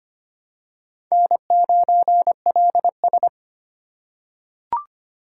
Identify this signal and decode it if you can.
Morse code, heard as a keyed tone, about 25 wpm, 700 Hz. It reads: N9LS
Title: USA Call Signs - 2x Character Spacing - 25wpm